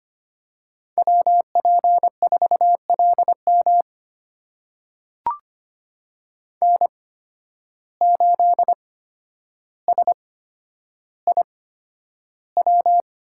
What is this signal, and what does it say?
Morse code, heard as a keyed tone, about 25 wpm, spaced as for 11 wpm, 700 Hz. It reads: WP4LM N8SIW
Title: USA Call Signs - 5 Characters - 8x Character Spacing - 25wpm